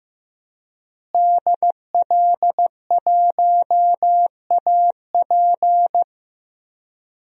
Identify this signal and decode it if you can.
Morse code, heard as a keyed tone, about 15 wpm, 700 Hz. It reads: DL1AP
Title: DXCC-100 Call Signs - 5 Characters - Repeated 2x - 15wpm